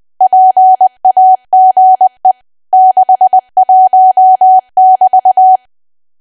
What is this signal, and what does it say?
Morse code, heard as a keyed tone, about 20 wpm, 750 Hz. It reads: PAGE 61<BT>
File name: cwa / W1AW-Level3 / 20WPM